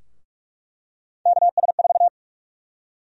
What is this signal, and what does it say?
Morse code, heard as a keyed tone, about 45 wpm, 700 Hz. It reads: KS4
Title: USA Call Sign Prefixes - Repeated 2x - 45wpm